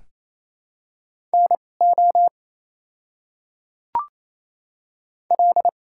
Morse code, heard as a keyed tone, about 28 wpm, spaced as for 20 wpm, 690 Hz. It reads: NO L
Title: 2x Character Spacing - Top 500 Words - 28wpm